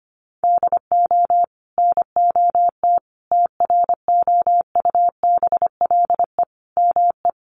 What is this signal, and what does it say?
Morse code, heard as a keyed tone, about 25 wpm, 700 Hz. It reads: DO NOT TROUBLE ME
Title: Sentences from Top 1200 Words 25wpm